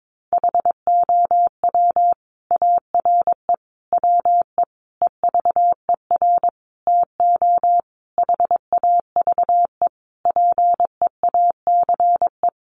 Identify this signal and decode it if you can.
Morse code, heard as a keyed tone, about 22 wpm, 700 Hz. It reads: HOW ARE WE EVER TO HAVE PEACE
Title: Sentences from Top 1500 Words 22wpm